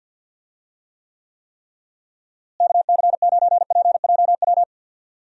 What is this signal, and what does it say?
Morse code, heard as a keyed tone, about 52 wpm, 700 Hz. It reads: KC9PJW
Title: USA Call Signs - 6 Characters - Repeated 3x - 50wpm